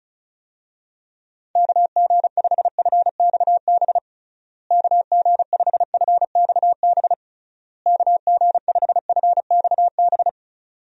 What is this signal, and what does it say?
Morse code, heard as a keyed tone, about 36 wpm, 700 Hz. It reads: KG5FXB KG5FXB KG5FXB
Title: USA Call Signs - 6 Characters - Repeated 3x - 35wpm